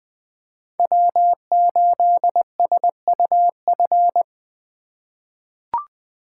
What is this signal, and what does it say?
Morse code, heard as a keyed tone, about 20 wpm, 700 Hz. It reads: W8SUF
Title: USA Call Signs - 3x Character Spacing - 20wpm